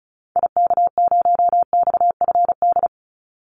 Morse code, heard as a keyed tone, about 34 wpm, 700 Hz. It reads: IK0XFD
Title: Call Signs - 35wpm